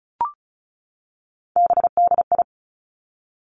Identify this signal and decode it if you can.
Morse code, heard as a keyed tone, about 36 wpm, 700 Hz. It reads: BDI
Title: Three Challenging Letter-Number Combinations 35wpm